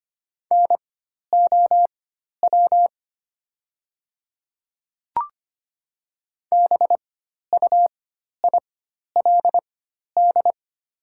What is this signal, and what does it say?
Morse code, heard as a keyed tone, about 25 wpm, spaced as for 12 wpm, 700 Hz. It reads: NOW BUILD
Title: 4x Character Spacing - Top 200 Words - 25wpm